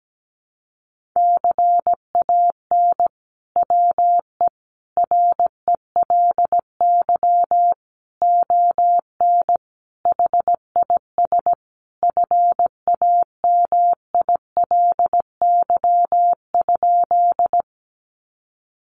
Morse code, heard as a keyed tone, about 17 wpm, 700 Hz. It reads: CAN WE RELY ON HIS FAMILY?
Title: Sentences from Top 1200 Words 17wpm